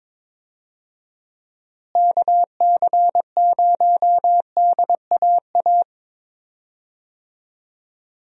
Morse code, heard as a keyed tone, about 22 wpm, 700 Hz. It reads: KC0DAA
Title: USA Call Signs - 6 Characters - Repeated 3x - 22wpm